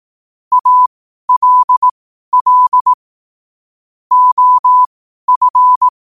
Morse code, heard as a keyed tone, about 18 wpm, spaced as for 12 wpm, 1000 Hz. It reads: ALL OF
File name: qotdfiles / 12wpm